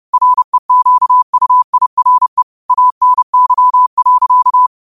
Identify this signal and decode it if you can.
Morse code, heard as a keyed tone, about 30 wpm, 1000 Hz. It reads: REQUIRE ANY.
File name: qotdfiles / 30wpm